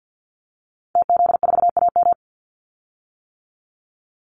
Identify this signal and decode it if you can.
Morse code, heard as a keyed tone, about 50 wpm, 700 Hz. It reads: TZ4AM
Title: Call Signs - 50wpm